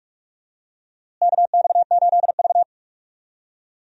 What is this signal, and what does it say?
Morse code, heard as a keyed tone, about 42 wpm, 700 Hz. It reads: KX8V
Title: USA Call Signs - 4 Characters - Repeated 2x - 45wpm